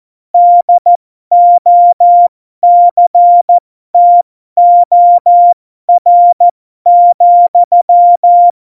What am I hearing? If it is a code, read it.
Morse code, heard as a keyed tone, about 14 wpm, spaced as for 12 wpm, 700 Hz. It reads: DOCTOR,
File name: Morsekurs / Witze